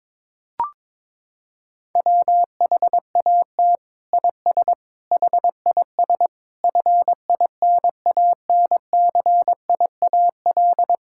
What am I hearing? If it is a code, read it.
Morse code, heard as a keyed tone, about 22 wpm, 700 Hz. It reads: WHAT IS HIS FINANCIAL
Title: Sentences from Top 1000 Words - Review Part 5 - 22wpm